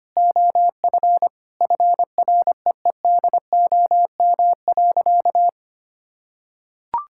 Morse code, heard as a keyed tone, about 25 wpm, 700 Hz. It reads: OF FREEDOM.
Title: Sentences from Top 1000 Words - Review Part 5 - 25wpm